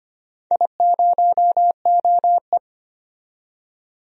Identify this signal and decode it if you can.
Morse code, heard as a keyed tone, about 25 wpm, 700 Hz. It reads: I0OE
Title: DXCC-100 Call Signs - 4 Characters - 25wpm